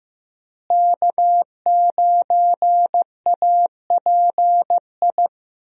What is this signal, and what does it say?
Morse code, heard as a keyed tone, about 15 wpm, 700 Hz. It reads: K9API
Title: USA Call Signs - 5 Characters - Repeated 3x - 15wpm